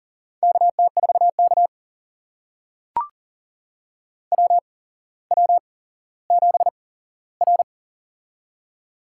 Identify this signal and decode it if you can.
Morse code, heard as a keyed tone, about 38 wpm, spaced as for 17 wpm, 700 Hz. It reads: KT4K WW7R
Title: USA Call Signs - 4 Characters - 8x Character Spacing - 40wpm